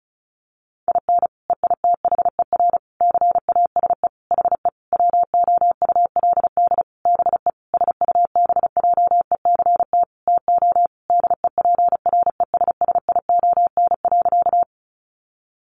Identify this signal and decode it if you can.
Morse code, heard as a keyed tone, about 34 wpm, 700 Hz. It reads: IN EITHER CASE HE WOULD BE SUBJECT TO DEPRESSION.
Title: Sentences from Top 1300 Words 35wpm